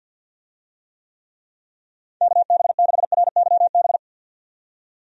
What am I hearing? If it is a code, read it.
Morse code, heard as a keyed tone, about 50 wpm, 700 Hz. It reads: KB6RYB
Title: USA Call Signs - 6 Characters - 50wpm